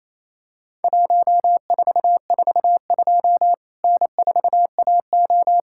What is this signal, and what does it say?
Morse code, heard as a keyed tone, about 28 wpm, 700 Hz. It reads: 1442 N4AO
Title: Worked All Europe DX Contest - QTC Exchange - 28wpm